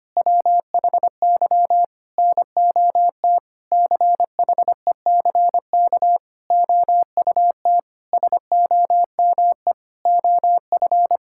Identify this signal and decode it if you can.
Morse code, heard as a keyed tone, about 25 wpm, 700 Hz. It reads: WHY NOT CHECK OUT SOME OF
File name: practice / 25wpm / Sentences with 9-Word Phrases - 25wpm